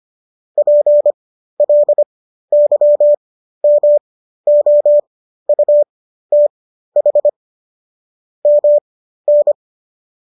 Morse code, heard as a keyed tone, about 25 wpm, spaced as for 13 wpm, 590 Hz. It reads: PLYMOUTH MN